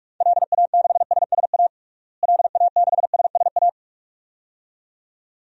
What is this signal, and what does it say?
Morse code, heard as a keyed tone, about 46 wpm, 700 Hz. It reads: LA6SSA LA6SSA
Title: DXCC-100 Call Signs - 6 Characters - Repeated 2x - 45wpm